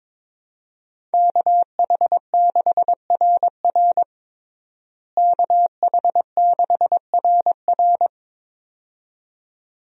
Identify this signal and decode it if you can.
Morse code, heard as a keyed tone, about 22 wpm, 700 Hz. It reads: KH6RR KH6RR
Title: USA Call Signs - 5 Characters - Repeated 2x - 22wpm